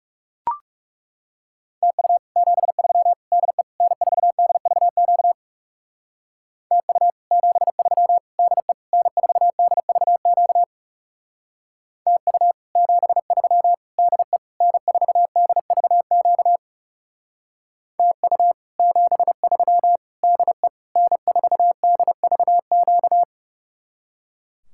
Morse code, heard as a keyed tone, about 37 wpm, 700 Hz. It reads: TU 73 DE N4DVQ TU 73 DE N4DVQ TU 73 DE N4DVQ TU 73 DE N4DVQ
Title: Speed-Racing Common QSO Phrases Encore - 30wpm